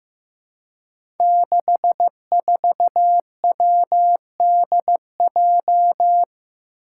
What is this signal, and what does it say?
Morse code, heard as a keyed tone, about 15 wpm, 700 Hz. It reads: 64WDJ
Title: Five Challenging Letter-Number Combinations 15wpm